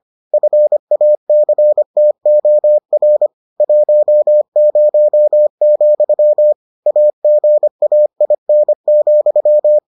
Morse code, heard as a keyed tone, about 25 wpm, 600 Hz. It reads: FACTOR 10, AGAIN,